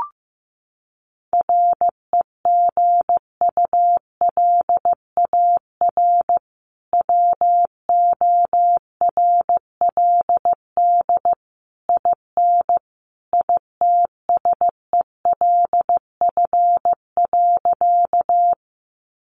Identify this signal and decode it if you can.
Morse code, heard as a keyed tone, about 15 wpm, 700 Hz. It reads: REGULAR WORLD IN ITSELF.
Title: Sentences from Top 900 Words 15wpm